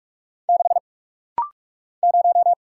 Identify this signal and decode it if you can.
Morse code, heard as a keyed tone, about 44 wpm, 700 Hz. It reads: 6 0
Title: Single Letter-Number Alt - Rapid-Fire 45wpm